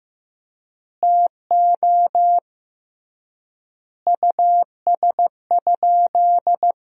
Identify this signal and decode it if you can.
Morse code, heard as a keyed tone, about 16 wpm, 700 Hz. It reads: TO US?
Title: 3x Word-Spacing - Sentences from Top 400 Words - 15wpm